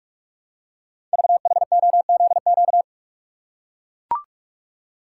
Morse code, heard as a keyed tone, about 44 wpm, 700 Hz. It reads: VHOZQ
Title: Code Groups - 5 Letters - 45wpm